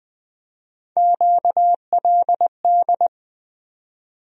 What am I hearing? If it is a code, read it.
Morse code, heard as a keyed tone, about 20 wpm, 700 Hz. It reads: QLD
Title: USA Call Sign Postfixes - 3 Letters 20wpm